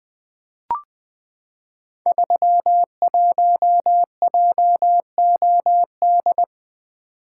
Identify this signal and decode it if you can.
Morse code, heard as a keyed tone, about 20 wpm, 700 Hz. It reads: 31JOD
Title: Code Groups - 5 Letter:Numbers - 20wpm